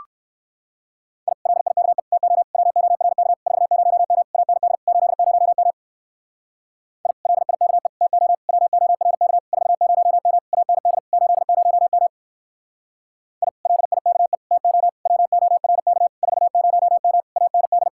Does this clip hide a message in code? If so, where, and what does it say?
Yes: Morse code from 1.3 to 17.9 s, heard as a keyed tone, about 58 wpm, 700 Hz. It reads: I LIKE TO WORK 40M AND 80M I LIKE TO WORK 40M AND 80M I LIKE TO WORK 40M AND